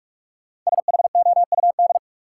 Morse code, heard as a keyed tone, about 44 wpm, 700 Hz. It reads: SHOUD
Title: Sentences from Top 1000 Words - Review Part 4 - New Word Drill - 45wpm